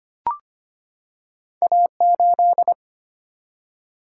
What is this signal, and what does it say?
Morse code, heard as a keyed tone, about 25 wpm, 700 Hz. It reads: A8
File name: Two Letter-Number Combinations - 25wpm